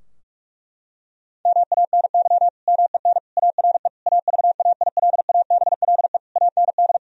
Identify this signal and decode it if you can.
Morse code, heard as a keyed tone, about 46 wpm, 700 Hz. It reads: MANY MEN ARE AVAILABLE AND
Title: Sentences from Top 1000 Words - Review Part 3 - 45wpm